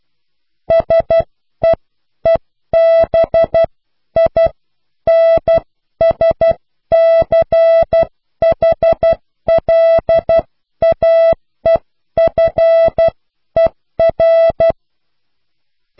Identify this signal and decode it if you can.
Morse code, heard as a keyed tone, about 12 wpm, spaced as for 10 wpm, 660 Hz. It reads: SEEBINSCHLAEFER
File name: cw / Pseudo woorden (15)